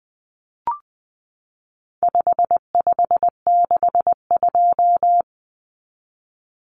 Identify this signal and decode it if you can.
Morse code, heard as a keyed tone, about 20 wpm, 700 Hz. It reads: 5562
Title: Common Cut Numbers - 4-Digit 20wpm